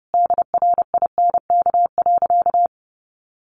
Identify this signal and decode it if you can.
Morse code, heard as a keyed tone, about 30 wpm, 700 Hz. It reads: DRINK.
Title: Sentences from Top 1500 Words 30wpm